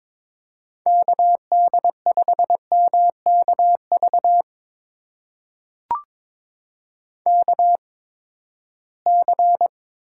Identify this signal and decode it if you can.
Morse code, heard as a keyed tone, about 22 wpm, spaced as for 15 wpm, 700 Hz. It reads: KD5MKV KC
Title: USA Call Signs - 6 Characters - 8x Character Spacing - 22wpm